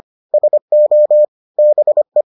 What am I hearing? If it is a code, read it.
Morse code, heard as a keyed tone, about 25 wpm, 600 Hz. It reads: SO BE